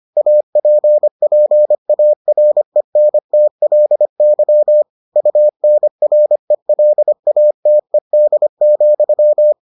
Morse code, heard as a keyed tone, about 25 wpm, 600 Hz. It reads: APPARENTLY UNRELATED,